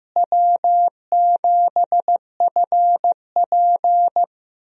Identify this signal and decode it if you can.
Morse code, heard as a keyed tone, about 15 wpm, 700 Hz. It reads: W7FP
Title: USA Call Signs - 4 Characters - Repeated 3x - 15wpm